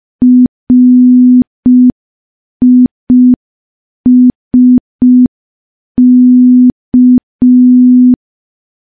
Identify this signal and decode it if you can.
Morse code, heard as a keyed tone, about 5 wpm, 250 Hz. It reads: RISK
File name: morse / morse-training-05WPM